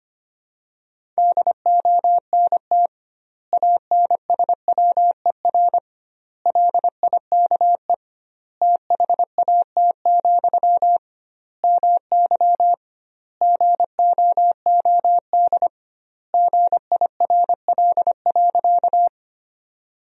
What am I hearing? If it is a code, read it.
Morse code, heard as a keyed tone, about 25 wpm, 700 Hz. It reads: DONT ANSWER LIKE THAT, MY GOOD GIRL.
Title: 2x Word-Spacing - Sentences from Top 400 Words - 25wpm